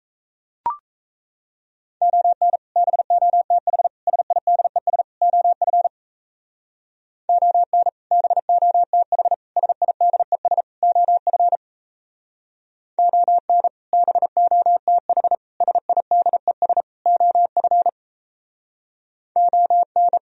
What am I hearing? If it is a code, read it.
Morse code, heard as a keyed tone, about 37 wpm, 700 Hz. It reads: ON BOTH SIDES OF ON BOTH SIDES OF ON BOTH SIDES OF ON